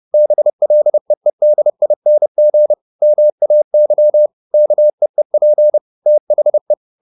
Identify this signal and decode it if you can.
Morse code, heard as a keyed tone, about 30 wpm, 600 Hz. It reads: BLEEDING MAY KEEP THE